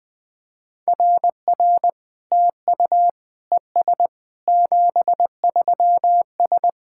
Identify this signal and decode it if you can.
Morse code, heard as a keyed tone, about 21 wpm, 700 Hz. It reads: RR TU ES 73S
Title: SOTA QSO Practice 20wpm